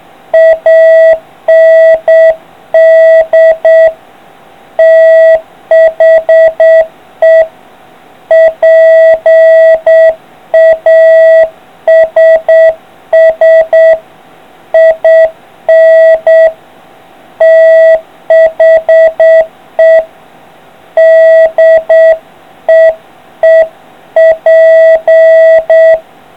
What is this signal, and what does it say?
Morse code, heard as a keyed tone, about 8 wpm, 650 Hz. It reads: AND THE PASS IN THE DEEP